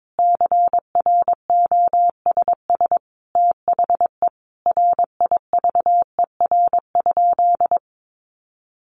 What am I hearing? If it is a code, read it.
Morse code, heard as a keyed tone, about 22 wpm, 700 Hz. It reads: CROSS THE RIVER?
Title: Sentences from Top 1400 Words 22wpm